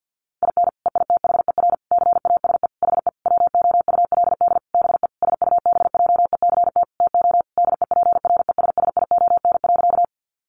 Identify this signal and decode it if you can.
Morse code, heard as a keyed tone, about 50 wpm, 700 Hz. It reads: IN EITHER CASE HE WOULD BE SUBJECT TO DEPRESSION.
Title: Sentences from Top 1300 Words 50wpm